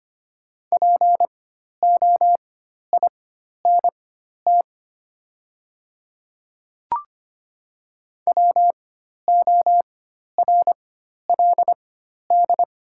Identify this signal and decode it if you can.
Morse code, heard as a keyed tone, about 25 wpm, spaced as for 12 wpm, 700 Hz. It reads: POINT WORLD
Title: 4x Character Spacing - Top 300 Words - 25wpm